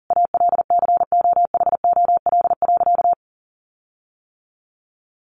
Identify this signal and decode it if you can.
Morse code, heard as a keyed tone, about 40 wpm, 700 Hz. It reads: ALCOHOL.
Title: Sentences from Top 1400 Words 40wpm